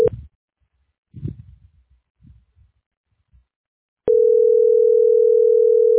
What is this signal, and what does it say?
Ringback tone (North America), 440 + 480 Hz.